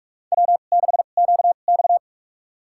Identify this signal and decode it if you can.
Morse code, heard as a keyed tone, about 44 wpm, spaced as for 32 wpm, 690 Hz. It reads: W6QX
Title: USA Call Signs - 4 Characters - 2x Character Spacing - 45wpm